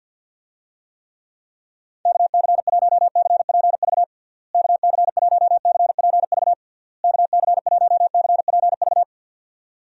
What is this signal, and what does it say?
Morse code, heard as a keyed tone, about 46 wpm, 700 Hz. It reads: KC1CPV KC1CPV KC1CPV
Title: USA Call Signs - 6 Characters - Repeated 3x - 50wpm